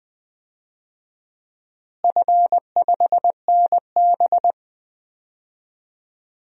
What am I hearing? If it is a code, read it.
Morse code, heard as a keyed tone, about 21 wpm, 700 Hz. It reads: F5NB
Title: DXCC-100 Call Signs - 4 Characters - Repeated 2x - 20wpm